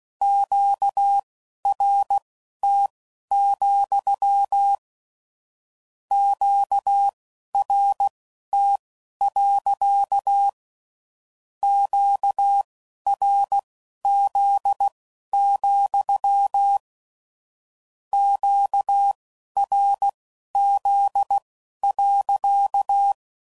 Morse code, heard as a keyed tone, about 16 wpm, spaced as for 11 wpm, 800 Hz. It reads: QRT, QRT. QRZ, QRZ.
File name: CW Code Groups @15WPM / Q Calls / Q Calls sppc